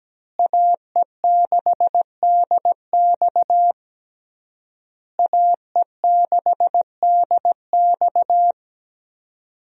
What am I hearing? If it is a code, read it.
Morse code, heard as a keyed tone, about 17 wpm, 700 Hz. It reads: AE6DX AE6DX
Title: USA Call Signs - 5 Characters - Repeated 3x - 17wpm